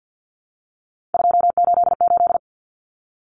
Morse code, heard as a keyed tone, about 50 wpm, 700 Hz. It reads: 288
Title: Common Cut Numbers - 3-Digit 50wpm